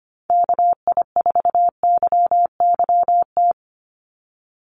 Morse code, heard as a keyed tone, about 25 wpm, 700 Hz. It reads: KI4YYT
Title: USA Call Signs 25wpm